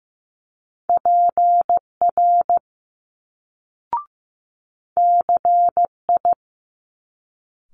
Morse code, heard as a keyed tone, about 15 wpm, 700 Hz. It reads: PR CI
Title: Multi-letter Phonemes Quick Fire 15wpm